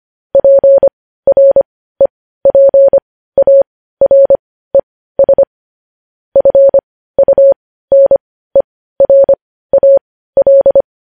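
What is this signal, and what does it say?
Morse code, heard as a keyed tone, about 25 wpm, spaced as for 15 wpm, 550 Hz. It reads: PREPARES FUNERAL
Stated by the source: Morse code news headlines at 15 wpm.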